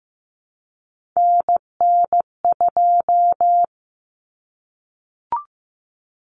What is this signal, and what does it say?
Morse code, heard as a keyed tone, about 15 wpm, 700 Hz. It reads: NN2
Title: Common Cut Numbers - 3-Digit 15wpm